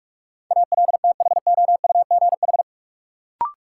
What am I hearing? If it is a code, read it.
Morse code, heard as a keyed tone, about 46 wpm, 700 Hz. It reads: ALTHOUGH
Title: Sentences from Top 600 Words - New Word Drill - 45wpm